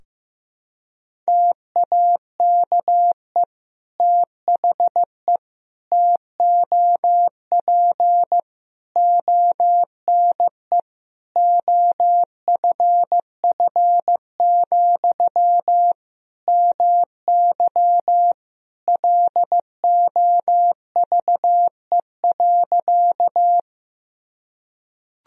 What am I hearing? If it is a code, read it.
Morse code, heard as a keyed tone, about 16 wpm, 700 Hz. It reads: TAKE THE TOP ONE OFF, MY LOVE.
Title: Sentences from Top 1000 Words - Review Part 2 - 15wpm